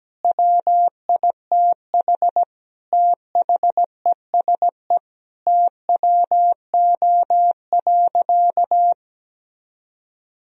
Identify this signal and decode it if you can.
Morse code, heard as a keyed tone, about 17 wpm, 700 Hz. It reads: WITH THESE TWO.
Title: Sentences from Top 1000 Words - Review Part 1 - 17wpm